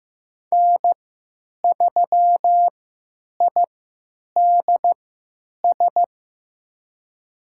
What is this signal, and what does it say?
Morse code, heard as a keyed tone, about 15 wpm, spaced as for 9 wpm, 700 Hz. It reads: N3IDS
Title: USA Call Signs - 3x Character Spacing - 15wpm